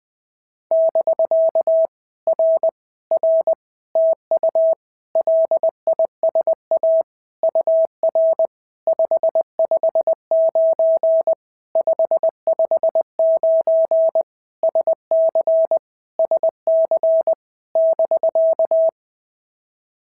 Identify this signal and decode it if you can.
Morse code, heard as a keyed tone, about 20 wpm, 650 Hz. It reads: <BK> R R TU LISA UR 559 559 SC SC <BK>
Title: POTA QSO Practice - 20wpm